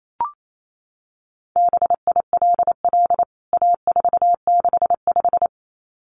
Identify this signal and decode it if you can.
Morse code, heard as a keyed tone, about 28 wpm, 700 Hz. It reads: BILL A465
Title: CWT Exchange 28wpm